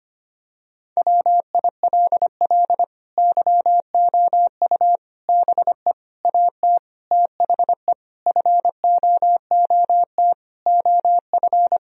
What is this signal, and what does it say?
Morse code, heard as a keyed tone, about 25 wpm, 700 Hz. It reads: WILL YOU BE AT THE FOOT OF
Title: Sentences from Top 1000 Words - Review Part 3 - 25wpm